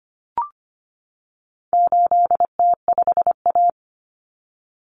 Morse code, heard as a keyed tone, about 25 wpm, 700 Hz. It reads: 8T5A
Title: Common Cut Numbers - 4-Digit 25wpm